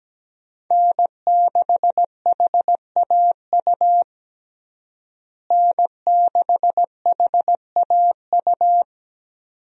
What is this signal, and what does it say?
Morse code, heard as a keyed tone, about 17 wpm, 700 Hz. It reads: N6HAU N6HAU
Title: USA Call Signs - 5 Characters - Repeated 2x - 17wpm